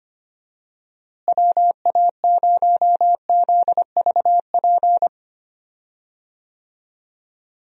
Morse code, heard as a keyed tone, about 25 wpm, 700 Hz. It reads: WA0ZVP
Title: USA Call Signs - 6 Characters - Repeated 2x - 25wpm